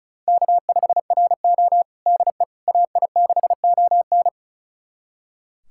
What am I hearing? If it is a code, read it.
Morse code, heard as a keyed tone, about 34 wpm, 700 Hz. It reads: K5RO DE AI6ON
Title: USA Callsign DE Callsign 35wpm